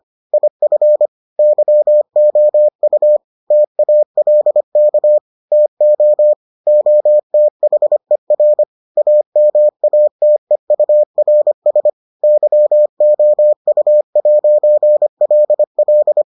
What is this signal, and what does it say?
Morse code, heard as a keyed tone, about 25 wpm, 600 Hz. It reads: IF YOU TALK TO OTHER AMATEURS YOU'LL